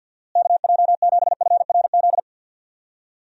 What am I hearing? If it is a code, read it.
Morse code, heard as a keyed tone, about 50 wpm, 700 Hz. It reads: KJ7FRZ
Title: USA Call Signs - 6 Characters - Repeated 3x - 50wpm